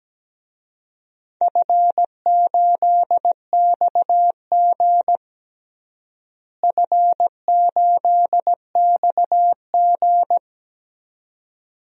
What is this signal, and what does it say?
Morse code, heard as a keyed tone, about 17 wpm, 700 Hz. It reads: F8XG F8XG
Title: DXCC-100 Call Signs - 4 Characters - Repeated 2x - 17wpm